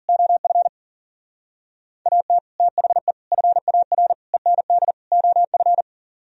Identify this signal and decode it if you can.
Morse code, heard as a keyed tone, about 50 wpm, 700 Hz. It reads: OF AT THE FAR END OF
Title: Speed-Racing - Sets of 5 Words 40wpm